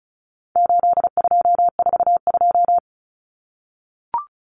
Morse code, heard as a keyed tone, about 34 wpm, 700 Hz. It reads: 8242
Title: Common Cut Numbers - 4-Digit 35wpm